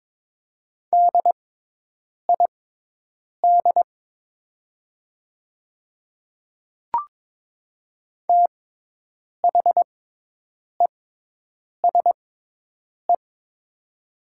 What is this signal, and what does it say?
Morse code, heard as a keyed tone, about 22 wpm, spaced as for 8 wpm, 700 Hz. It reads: DID THESE
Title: 6x Character Spacing - Top 200 Words - 22wpm